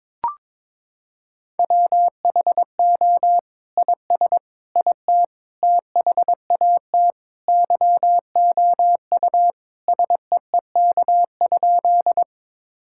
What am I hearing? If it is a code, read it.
Morse code, heard as a keyed tone, about 22 wpm, 700 Hz. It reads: WHO IS IT THAT YOU SEEK?
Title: Sentences from Top 1600 Words 22wpm